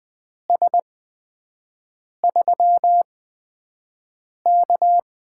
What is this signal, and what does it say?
Morse code, heard as a keyed tone, about 20 wpm, spaced as for 5 wpm, 700 Hz. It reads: S3K
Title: USA Call Signs - 8x Character Spacing - 20wpm